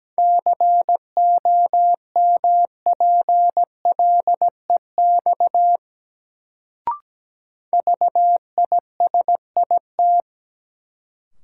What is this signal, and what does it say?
Morse code, heard as a keyed tone, about 17 wpm, 700 Hz. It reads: COMPLEX VISIT
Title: Sentences from Top 900 Words - New Word Drill - 17wpm